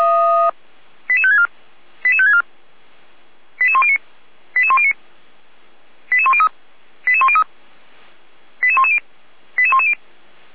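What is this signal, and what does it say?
Five-tone selective call (ZVEI), heard as quick runs of five tones, tones of about 70 ms. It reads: 88564 88564 88109 88109 88103 88103 88110 88110
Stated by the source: ZVEI five-tone selective call received by SDR.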